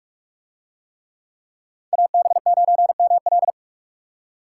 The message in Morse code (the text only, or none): AB9ML